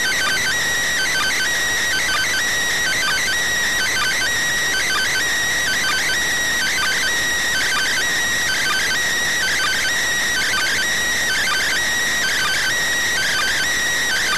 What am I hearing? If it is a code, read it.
Two-tone teleprinter signal, about 400 Hz shift.